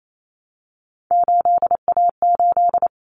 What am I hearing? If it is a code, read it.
Morse code, heard as a keyed tone, about 28 wpm, 700 Hz. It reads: 8A8
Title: Common Cut Numbers - 3-Digit 28wpm